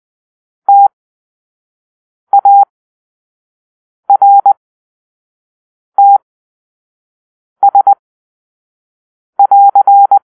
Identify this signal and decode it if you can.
Morse code, heard as a keyed tone, about 20 wpm, spaced as for 5 wpm, 800 Hz. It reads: TARTS<AR>
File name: arrl / Grp-2 Drill-4